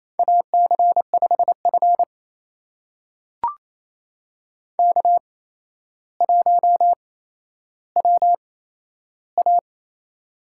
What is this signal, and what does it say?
Morse code, heard as a keyed tone, about 28 wpm, spaced as for 12 wpm, 700 Hz. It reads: AC5F K1WA
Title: USA Call Signs - 4 Characters - 8x Character Spacing - 28wpm